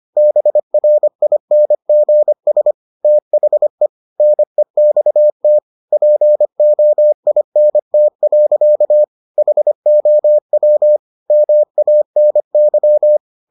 Morse code, heard as a keyed tone, about 25 wpm, 600 Hz. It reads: BRINGS THE NEXT POINT. HOW MANY